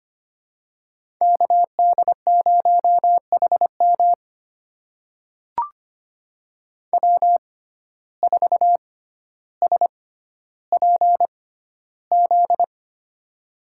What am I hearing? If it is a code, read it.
Morse code, heard as a keyed tone, about 25 wpm, spaced as for 13 wpm, 700 Hz. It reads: KD0HM W4SPZ
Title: USA Call Signs - 5 Characters - 6x Character Spacing - 25wpm